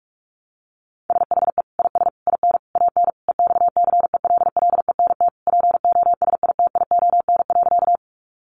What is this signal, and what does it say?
Morse code, heard as a keyed tone, about 44 wpm, 700 Hz. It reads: SHE IS IN AN EXCELLENT POSITION.
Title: Sentences from Top 1000 Words 45wpm